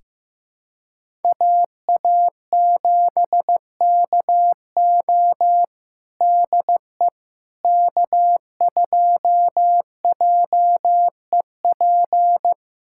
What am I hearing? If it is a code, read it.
Morse code, heard as a keyed tone, about 15 wpm, 700 Hz. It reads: AA7KO DE K2JEP
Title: USA Callsign DE Callsign - 5 Characters - 15wpm